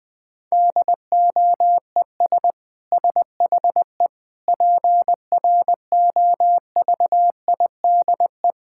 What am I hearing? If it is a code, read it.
Morse code, heard as a keyed tone, about 20 wpm, 700 Hz. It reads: DOES SHE PROVIDE
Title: Sentences from Top 1000 Words - Review Part 6 - 20wpm